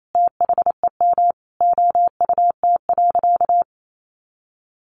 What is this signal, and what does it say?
Morse code, heard as a keyed tone, about 28 wpm, 700 Hz. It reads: THEM OUT.
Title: Sentences from Top 500 Words 28wpm